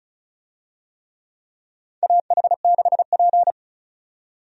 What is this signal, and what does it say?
Morse code, heard as a keyed tone, about 34 wpm, 700 Hz. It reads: AH6P
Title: USA Call Signs - 4 Characters - 35wpm